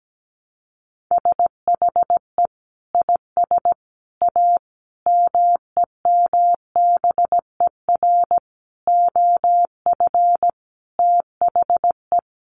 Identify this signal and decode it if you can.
Morse code, heard as a keyed tone, about 17 wpm, 700 Hz. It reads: SHE IS A MEMBER OF THE
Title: Sentences from Top 1100 Words 17wpm